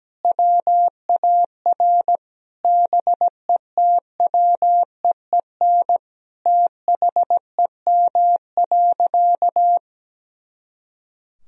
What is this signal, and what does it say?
Morse code, heard as a keyed tone, about 17 wpm, 700 Hz. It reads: WAR BETWEEN THEM.
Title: Sentences from Top 400 Words and Abbreviations 17wpm